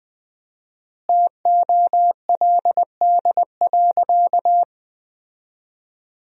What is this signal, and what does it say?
Morse code, heard as a keyed tone, about 21 wpm, 700 Hz. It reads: TOLD.
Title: 4x Word-Spacing - Sentences from Top 400 Words - 20wpm